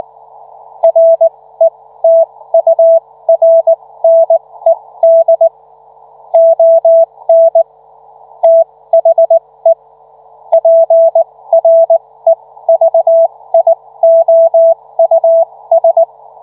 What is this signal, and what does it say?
Morse code, heard as a keyed tone, about 19 wpm, spaced as for 15 wpm, 660 Hz. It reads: RETURNED ON THE PREVIOUS